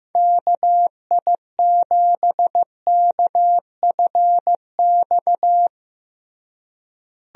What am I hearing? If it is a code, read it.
Morse code, heard as a keyed tone, about 15 wpm, 700 Hz. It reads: KI7KFX
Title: USA Call Signs - 6 Characters - Repeated 3x - 15wpm